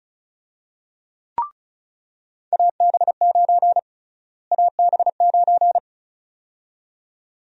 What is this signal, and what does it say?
Morse code, heard as a keyed tone, about 34 wpm, 700 Hz. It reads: AB9 AB9
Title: USA Call Sign Prefixes - Repeated 2x - 35wpm